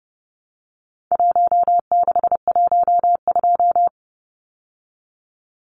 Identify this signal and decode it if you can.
Morse code, heard as a keyed tone, about 30 wpm, 700 Hz. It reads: 1612
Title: Numbers - Contest Style - 4-Digit 30wpm